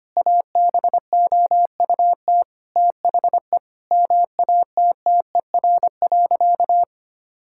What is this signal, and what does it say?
Morse code, heard as a keyed tone, about 25 wpm, 700 Hz. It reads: ABOUT THE MATTER.
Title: Sentences from Top 1000 Words - Review Part 5 - 25wpm